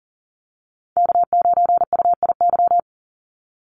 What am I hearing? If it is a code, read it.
Morse code, heard as a keyed tone, about 40 wpm, 700 Hz. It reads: K9UIY
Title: Call Signs - 40wpm